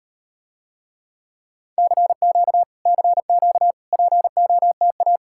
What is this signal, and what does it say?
Morse code, heard as a keyed tone, about 38 wpm, 700 Hz. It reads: CQ CQ POTA